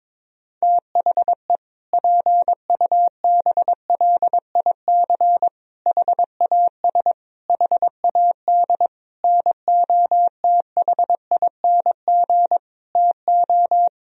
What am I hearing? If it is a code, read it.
Morse code, heard as a keyed tone, about 22 wpm, 700 Hz. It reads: THE PUBLIC HAS HAD NOTHING TO
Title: Sentences from Top 1000 Words - Review Part 3 - 22wpm